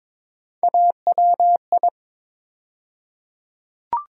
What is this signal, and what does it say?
Morse code, heard as a keyed tone, about 22 wpm, 700 Hz. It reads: AWI
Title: USA Call Sign Postfixes - 3 Letters 22wpm